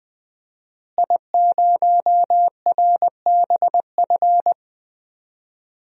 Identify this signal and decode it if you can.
Morse code, heard as a keyed tone, about 19 wpm, 700 Hz. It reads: I0RBF
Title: DXCC-100 Call Signs - 5 Characters - Repeated 2x - 20wpm